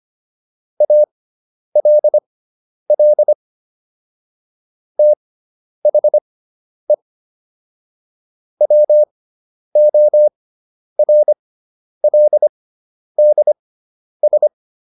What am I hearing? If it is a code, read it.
Morse code, heard as a keyed tone, about 25 wpm, spaced as for 10 wpm, 600 Hz. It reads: ALL THE WORLDS
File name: SS213 10